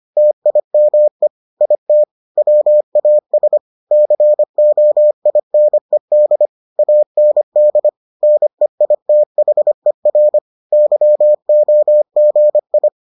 Morse code, heard as a keyed tone, about 25 wpm, 600 Hz. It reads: TIME IT WAS COINED AND NEITHER YOGI